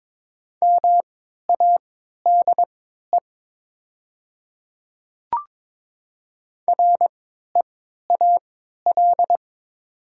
Morse code, heard as a keyed tone, about 22 wpm, spaced as for 12 wpm, 700 Hz. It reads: MADE REAL
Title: 3x Character Spacing - Top 300 Words - 22wpm